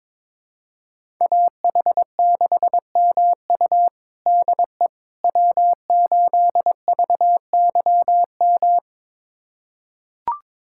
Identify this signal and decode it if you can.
Morse code, heard as a keyed tone, about 22 wpm, 700 Hz. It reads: AH6MU DE W8VYM
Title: USA Callsign DE Callsign - 5 Characters - 22wpm